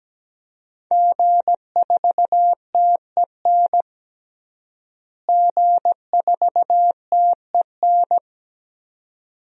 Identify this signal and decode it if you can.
Morse code, heard as a keyed tone, about 17 wpm, 700 Hz. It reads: G4TEN G4TEN
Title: DXCC-100 Call Signs - 5 Characters - Repeated 2x - 17wpm